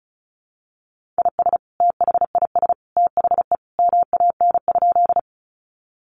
Morse code, heard as a keyed tone, about 34 wpm, 700 Hz. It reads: IS THIS THE MAN?
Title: Sentences from Top 200 Words 35wpm